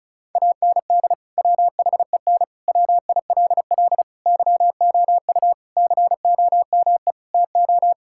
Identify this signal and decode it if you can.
Morse code, heard as a keyed tone, about 36 wpm, 700 Hz. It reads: AND WHEN WILL YOU COME TO
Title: Sentences from Top 1600 Words 35wpm